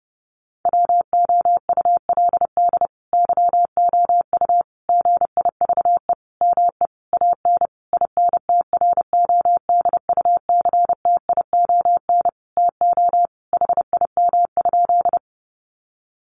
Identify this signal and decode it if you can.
Morse code, heard as a keyed tone, about 30 wpm, 700 Hz. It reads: WOULD YOU GIVE ME AN INTRODUCTION TO HIM?
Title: Sentences from Top 1000 Words 30wpm